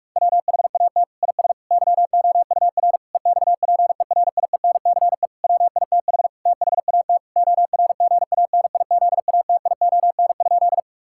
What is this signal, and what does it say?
Morse code, heard as a keyed tone, about 46 wpm, 700 Hz. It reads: WHAT IS YOUR EXPERIENCE WITH THAT ORGANIZATION?
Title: Sentences from Top 1000 Words - Review Part 5 - 45wpm